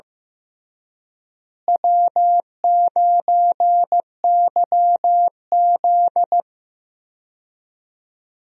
Morse code, heard as a keyed tone, about 15 wpm, 700 Hz. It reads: W9YZ
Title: USA Call Signs - 4 Characters - Repeated 3x - 15wpm